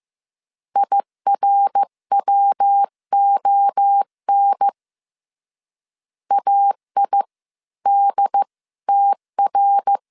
Morse code, heard as a keyed tone, about 15 wpm, spaced as for 13 wpm, 800 Hz. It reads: IRWON AIDTR
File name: CW / DET / 1 3 1500C S A 14wpm 2447